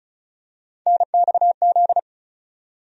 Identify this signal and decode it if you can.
Morse code, heard as a keyed tone, about 34 wpm, 700 Hz. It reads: NXZ
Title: USA Call Sign Postfixes - 3 Letters 35wpm